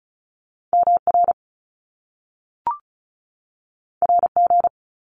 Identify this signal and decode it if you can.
Morse code, heard as a keyed tone, about 36 wpm, 700 Hz. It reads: MR RG
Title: Top Two Letter Combinations in Words 35wpm